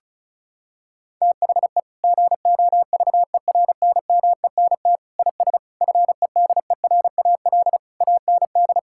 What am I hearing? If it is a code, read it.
Morse code, heard as a keyed tone, about 36 wpm, 700 Hz. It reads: THE GOVERNMENT IS FEDERAL AND
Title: Sentences from Top 1000 Words - Review Part 5 - 35wpm